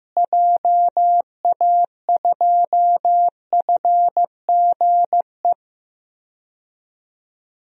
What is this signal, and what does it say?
Morse code, heard as a keyed tone, about 15 wpm, 700 Hz. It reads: JA2FGE
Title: DXCC-100 Call Signs - 6 Characters - Repeated 2x - 15wpm